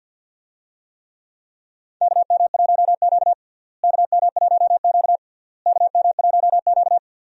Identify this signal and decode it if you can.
Morse code, heard as a keyed tone, about 48 wpm, 700 Hz. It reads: KM1Q KM1Q KM1Q
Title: USA Call Signs - 4 Characters - Repeated 3x - 50wpm